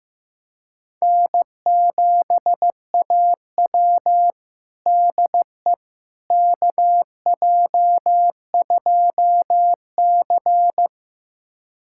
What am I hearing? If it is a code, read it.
Morse code, heard as a keyed tone, about 15 wpm, 700 Hz. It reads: N7AW DE KJ2C
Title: USA Callsign DE Callsign - 4 Characters - 15wpm